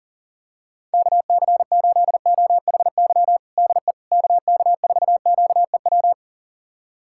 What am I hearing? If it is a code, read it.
Morse code, heard as a keyed tone, about 40 wpm, 700 Hz. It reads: KC8OHY DE KK4QEW
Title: USA Callsign DE Callsign 40wpm